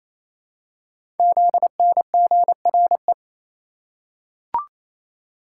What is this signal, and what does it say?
Morse code, heard as a keyed tone, about 27 wpm, 700 Hz. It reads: ZNGRE
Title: Code Groups - 5 Letters - 28wpm